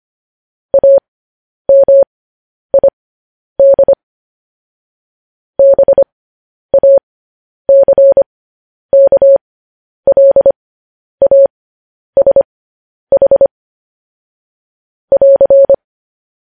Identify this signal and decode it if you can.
Morse code, heard as a keyed tone, about 25 wpm, spaced as for 10 wpm, 550 Hz. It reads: AMID BACKLASH <AR>
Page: news-headlines-in-morse-code-at-10-wpm